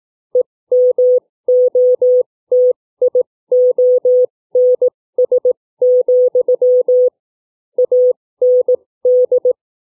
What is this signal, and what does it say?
Morse code, heard as a keyed tone, about 18 wpm, spaced as for 15 wpm, 500 Hz. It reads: EMOTIONS, AND
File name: CW / book reading